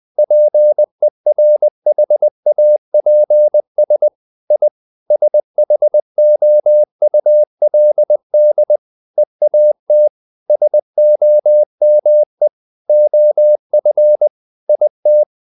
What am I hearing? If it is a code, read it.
Morse code, heard as a keyed tone, about 20 wpm, 600 Hz. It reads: PERHAPS I SHOULD EAT SOME OF IT
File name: the old man and the sea part 2 20wpm